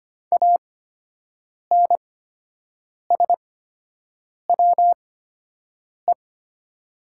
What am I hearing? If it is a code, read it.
Morse code, heard as a keyed tone, about 25 wpm, spaced as for 7 wpm, 690 Hz. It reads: ANSWE
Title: 8x Character Spacing - Top 500 Words - 25wpm